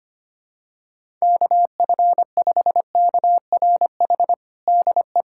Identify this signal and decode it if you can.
Morse code, heard as a keyed tone, about 25 wpm, 700 Hz. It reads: KF5KRH DE
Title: USA Callsign DE Callsign 25wpm